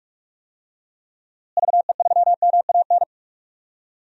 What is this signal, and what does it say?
Morse code, heard as a keyed tone, about 46 wpm, 700 Hz. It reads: VE3MAN
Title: DXCC-100 Call Signs - 6 Characters - Repeated 2x - 45wpm